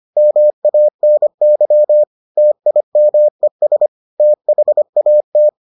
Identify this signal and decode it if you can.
Morse code, heard as a keyed tone, about 25 wpm, 600 Hz. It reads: MANY TIMES THAT